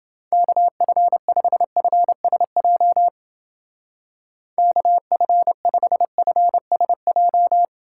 Morse code, heard as a keyed tone, about 28 wpm, 700 Hz. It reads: KF5FSJ KF5FSJ